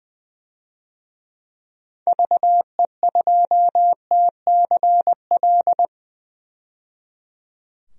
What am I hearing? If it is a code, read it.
Morse code, heard as a keyed tone, about 20 wpm, 700 Hz. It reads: VE2TCL
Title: DXCC-100 Call Signs - 6 Characters - 20wpm